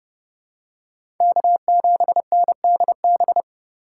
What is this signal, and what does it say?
Morse code, heard as a keyed tone, about 30 wpm, 700 Hz. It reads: K7NDB
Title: USA Call Signs - 5 Characters - 4x Character Spacing - 30wpm